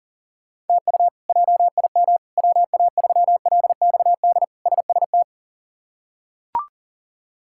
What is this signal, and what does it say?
Morse code, heard as a keyed tone, about 40 wpm, 700 Hz. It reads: TU JIM WA3LXD SST
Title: K1USN SST QSO Practice - 40wpm